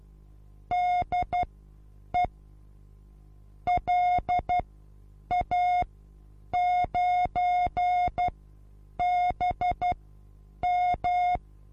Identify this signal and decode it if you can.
Morse code, heard as a keyed tone, about 12 wpm, spaced as for 8 wpm, 720 Hz. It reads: DE LA9BM
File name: repos / Morsekurs / DJ4UF Morsen23